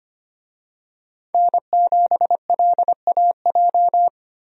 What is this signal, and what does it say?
Morse code, heard as a keyed tone, about 25 wpm, 700 Hz. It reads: N7LAJ
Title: USA Call Signs - 5 Characters - 25wpm